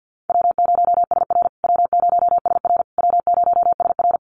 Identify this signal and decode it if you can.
Morse code, heard as a keyed tone, about 50 wpm, 700 Hz. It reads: W0SR W0SR W0SR
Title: USA Call Signs Repeated 3x 50wpm